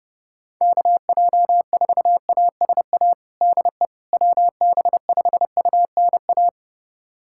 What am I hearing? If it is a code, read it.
Morse code, heard as a keyed tone, about 32 wpm, 700 Hz. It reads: KJ4ASA DE WB5UNA
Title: USA Callsign DE Callsign - 6 Characters - 30wpm